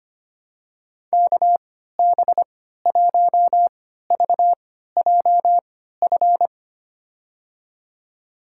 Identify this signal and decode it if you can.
Morse code, heard as a keyed tone, about 25 wpm, spaced as for 14 wpm, 700 Hz. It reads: KB1VJF
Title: USA Call Signs - 6 Characters - 3x Character Spacing - 25wpm